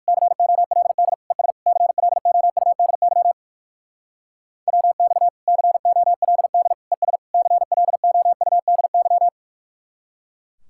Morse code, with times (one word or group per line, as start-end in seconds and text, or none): 0.1-1.1 COLD
1.3-1.5 ES
1.7-3.3 CLOUDY
4.7-5.3 WX
5.5-6.7 COLD
6.9-7.2 ES
7.3-9.3 CLOUDY